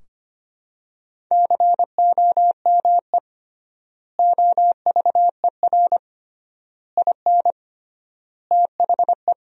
Morse code, heard as a keyed tone, about 25 wpm, 700 Hz. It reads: COME OVER IN THE
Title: 3x Word-Spacing - Sentences from Top 300 Words - 25wpm